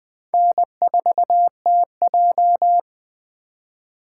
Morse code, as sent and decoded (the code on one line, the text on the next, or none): -. ....- - .---
N4TJ